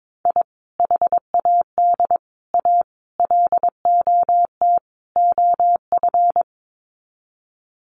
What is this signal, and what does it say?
Morse code, heard as a keyed tone, about 22 wpm, 700 Hz. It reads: I HAD A LOT OF
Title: Sets of 5 Words Encore 22wpm